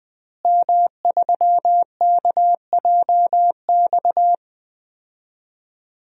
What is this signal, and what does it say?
Morse code, heard as a keyed tone, about 20 wpm, 700 Hz. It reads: M3KJX
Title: DXCC-100 Call Signs - 5 Characters - Repeated 3x - 20wpm